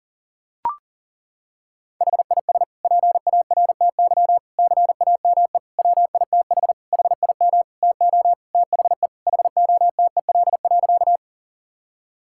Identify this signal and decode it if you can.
Morse code, heard as a keyed tone, about 37 wpm, 700 Hz. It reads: HIS PARTY CAME WITH HIM TO THE HOTEL.
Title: Sentences from Top 1600 Words 40wpm